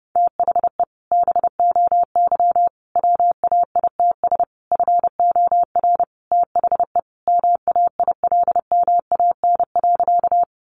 Morse code, heard as a keyed tone, about 30 wpm, 700 Hz. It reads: THE BOY WAITS FOR THE MAILMAN.
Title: Additional Easy Phrases 30wpm